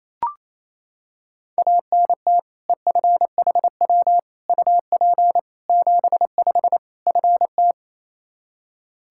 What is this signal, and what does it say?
Morse code, heard as a keyed tone, about 28 wpm, 700 Hz. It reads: ANT EFHW UP 75 FT
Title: Practice QSO - Exchange 2 - Info - 28wpm